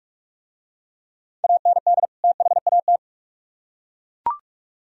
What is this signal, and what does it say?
Morse code, heard as a keyed tone, about 42 wpm, 700 Hz. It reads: AND THAT